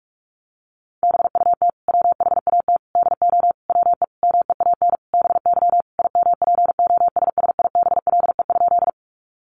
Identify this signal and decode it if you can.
Morse code, heard as a keyed tone, about 44 wpm, 700 Hz. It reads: BUT WHAT DO WE MEAN BY IMPOSSIBLE?
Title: Sentences from Top 1100 Words 45wpm